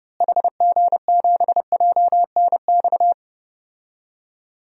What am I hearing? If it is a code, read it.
Morse code, heard as a keyed tone, about 30 wpm, 700 Hz. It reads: HG7JNX
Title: DXCC-100 Call Signs - 6 Characters - Repeated 3x - 30wpm